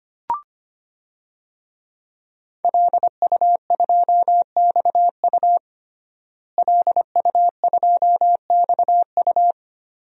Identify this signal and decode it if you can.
Morse code, heard as a keyed tone, about 25 wpm, 700 Hz. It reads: LU2XU LU2XU
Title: DXCC-100 Call Signs - 5 Characters - Repeated 2x - 25wpm